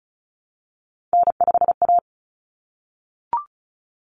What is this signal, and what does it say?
Morse code, heard as a keyed tone, about 34 wpm, 700 Hz. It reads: N5A
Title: Common Cut Numbers - 3-Digit 35wpm